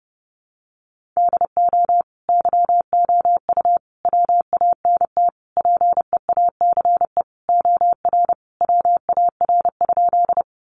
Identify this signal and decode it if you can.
Morse code, heard as a keyed tone, about 30 wpm, 700 Hz. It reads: DO YOU WANT PEACE OR WAR?
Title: Sentences from Top 1500 Words 30wpm